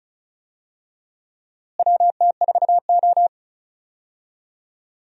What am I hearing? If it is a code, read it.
Morse code, heard as a keyed tone, about 34 wpm, 700 Hz. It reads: WT4O
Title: USA Call Signs - 4 Characters - Repeated 2x - 35wpm